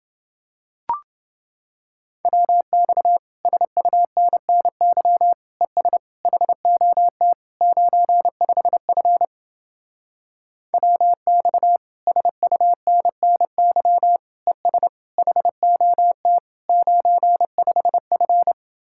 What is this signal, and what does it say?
Morse code, heard as a keyed tone, about 30 wpm, 700 Hz. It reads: WX SUNNY ES HOT 95F WX SUNNY ES HOT 95F